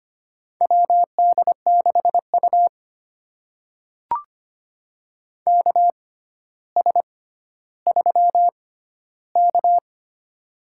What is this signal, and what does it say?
Morse code, heard as a keyed tone, about 25 wpm, spaced as for 13 wpm, 700 Hz. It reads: WD6U KS3K
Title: USA Call Signs - 4 Characters - 6x Character Spacing - 25wpm